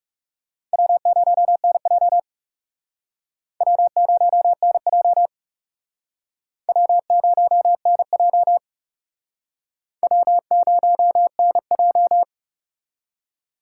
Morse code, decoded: W0NJ W0NJ W0NJ W0NJ

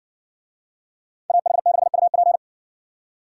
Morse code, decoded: AH6UP

